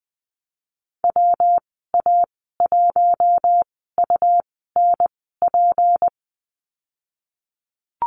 Morse code, heard as a keyed tone, about 20 wpm, spaced as for 15 wpm, 700 Hz. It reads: WA1UNP
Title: 179 Lesson 29g - Slash - Call Signs 20wpm